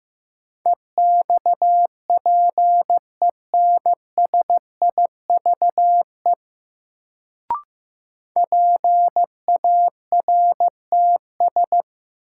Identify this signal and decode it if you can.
Morse code, heard as a keyed tone, about 16 wpm, 700 Hz. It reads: EXPENSIVE PARTS
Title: Sentences from Top 1000 Words - Review Part 6 - New Word Drill - 15wpm